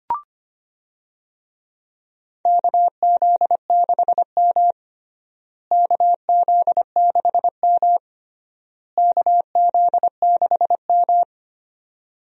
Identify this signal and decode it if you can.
Morse code, heard as a keyed tone, about 25 wpm, 700 Hz. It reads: KZ6M KZ6M KZ6M
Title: USA Call Signs - 4 Characters - Repeated 3x - 25wpm